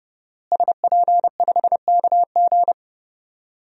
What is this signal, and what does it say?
Morse code, heard as a keyed tone, about 30 wpm, 700 Hz. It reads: SP5KG
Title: DXCC-100 Call Signs - 5 Characters - Repeated 2x - 30wpm